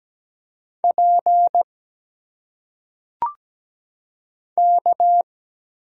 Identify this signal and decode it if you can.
Morse code, heard as a keyed tone, about 17 wpm, 700 Hz. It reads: P K
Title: Single Letters - Phonetic Pronunciation 17wpm